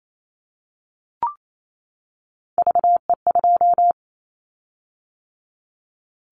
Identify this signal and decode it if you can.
Morse code, heard as a keyed tone, about 28 wpm, 700 Hz. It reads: VE2
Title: Callsign Prefixes for Top 50 DXCC - 28wpm